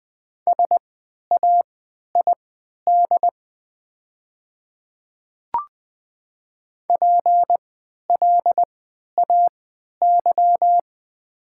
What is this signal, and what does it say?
Morse code, heard as a keyed tone, about 20 wpm, spaced as for 11 wpm, 700 Hz. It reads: SAID PLAY
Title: 3x Character Spacing - Top 200 Words - 20wpm